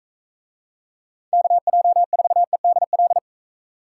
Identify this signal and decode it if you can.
Morse code, heard as a keyed tone, about 42 wpm, 700 Hz. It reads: KJ4EDL